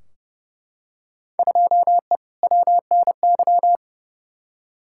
Morse code, heard as a keyed tone, about 31 wpm, 700 Hz. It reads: 2E WNY
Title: Field Day Exchanges Encore - 30wpm